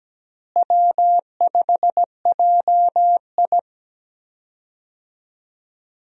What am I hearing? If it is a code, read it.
Morse code, heard as a keyed tone, about 17 wpm, 700 Hz. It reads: W5JI
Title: USA Call Signs - 4 Characters - Repeated 3x - 17wpm